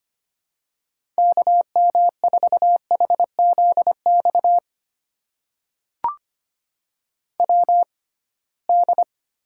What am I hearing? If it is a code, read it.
Morse code, heard as a keyed tone, about 25 wpm, spaced as for 19 wpm, 700 Hz. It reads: KM4HZX WD
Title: USA Call Signs - 6 Characters - 6x Character Spacing - 25wpm